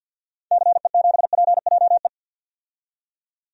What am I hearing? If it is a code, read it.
Morse code, heard as a keyed tone, about 51 wpm, 700 Hz. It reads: CE7PJE
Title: DXCC-100 Call Signs - 6 Characters - 50wpm